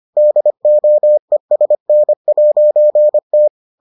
Morse code, heard as a keyed tone, about 25 wpm, 600 Hz. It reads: DOESN'T